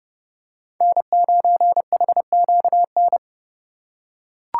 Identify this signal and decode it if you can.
Morse code, heard as a keyed tone, about 30 wpm, 700 Hz. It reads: N9HQN
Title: USA Call Signs - 5 Characters - 3x Character Spacing - 30wpm